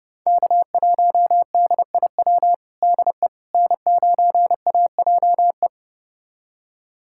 Morse code, heard as a keyed tone, about 30 wpm, 700 Hz. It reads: K1DIW DE N9AJE
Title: USA Callsign DE Callsign - 5 Characters - 30wpm